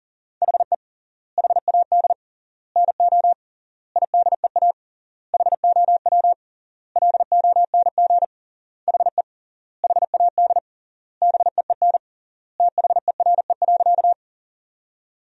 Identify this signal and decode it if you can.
Morse code, heard as a keyed tone, about 40 wpm, 700 Hz. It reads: HE HAD NO IDEA HOW LONG HE HAD BEEN THERE.
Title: 3x Word-Spacing - Sentences from Top 300 Words - 40wpm